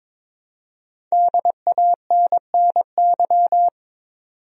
Morse code, heard as a keyed tone, about 22 wpm, 700 Hz. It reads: DANNY
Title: USA Common Names - 22wpm